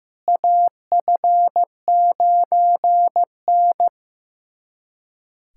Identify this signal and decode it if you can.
Morse code, heard as a keyed tone, about 15 wpm, 700 Hz. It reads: AF9N
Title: USA Callsign DE Callsign - 4 Characters - 15wpm